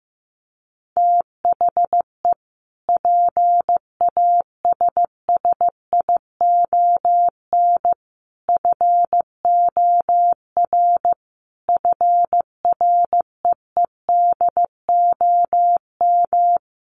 Morse code, heard as a keyed tone, about 15 wpm, 700 Hz. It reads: THE PASSION FOR FREEDOM
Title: Sentences from Top 1300 Words 15wpm